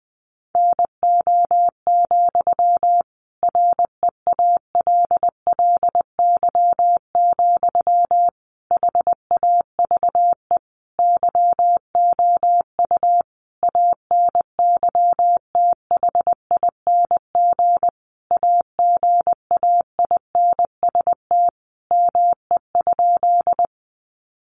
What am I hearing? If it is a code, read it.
Morse code, heard as a keyed tone, about 20 wpm, 700 Hz. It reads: NO, REALLY, HAVE YOU ANYTHING AGAINST ME?
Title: Sentences from Top 1000 Words 20wpm